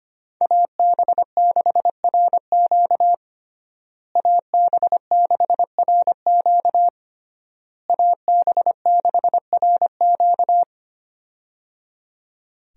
Morse code, heard as a keyed tone, about 25 wpm, 700 Hz. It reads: AB6RQ AB6RQ AB6RQ
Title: USA Call Signs - 5 Characters - Repeated 3x - 25wpm